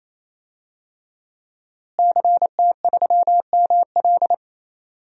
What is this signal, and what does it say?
Morse code, heard as a keyed tone, about 28 wpm, 700 Hz. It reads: CT3ML
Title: DXCC-100 Call Signs - 5 Characters - Repeated 3x - 28wpm